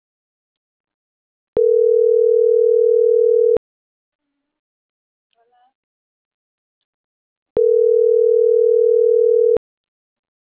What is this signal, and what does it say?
Ringback tone (North America), 440 + 480 Hz.